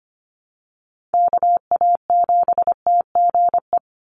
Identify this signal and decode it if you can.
Morse code, heard as a keyed tone, about 25 wpm, 700 Hz. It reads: KA7TGE
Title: USA Call Signs 25wpm